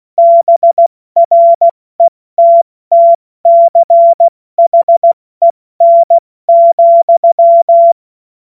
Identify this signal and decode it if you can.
Morse code, heard as a keyed tone, about 16 wpm, 680 Hz. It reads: BRETTCHEN,
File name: Morsekurs / Witze / Witz 20. Januar 2026 70 F80